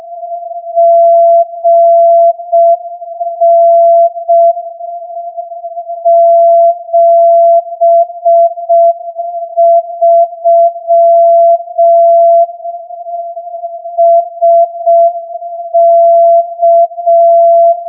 Morse code, heard as a keyed tone, about 6 wpm, 680 Hz. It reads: GN 73 SK
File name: 8feb AGC CW